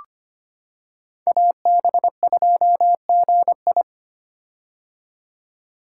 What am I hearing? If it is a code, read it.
Morse code, heard as a keyed tone, about 25 wpm, 700 Hz. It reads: AB2GI
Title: K1USN SST QSO Practice - 25wpm